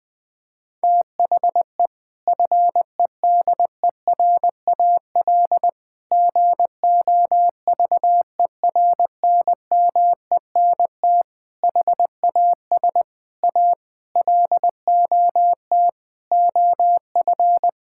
Morse code, heard as a keyed tone, about 20 wpm, 700 Hz. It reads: THE FEDERAL GOVERNMENT HAS A LOT OF
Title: Sentences from Top 1000 Words - Review Part 5 - 20wpm